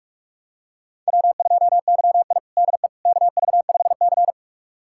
Morse code, heard as a keyed tone, about 44 wpm, 700 Hz. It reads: W2YI DE KV5C
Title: USA Callsign DE Callsign - 4 Characters - 45wpm